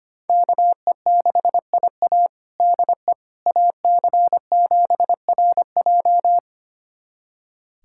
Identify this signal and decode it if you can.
Morse code, heard as a keyed tone, about 25 wpm, 700 Hz. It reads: KE6IA DE AC7RJ
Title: USA Callsign DE Callsign - 5 Characters - 25wpm